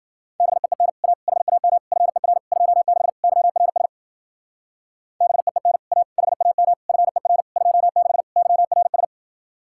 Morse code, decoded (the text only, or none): BEEN A HAM FER 26 YRS BEEN A HAM FER 26 YRS